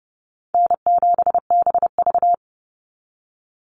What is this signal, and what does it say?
Morse code, heard as a keyed tone, about 30 wpm, 700 Hz. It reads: N7BV
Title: Call Signs - 30wpm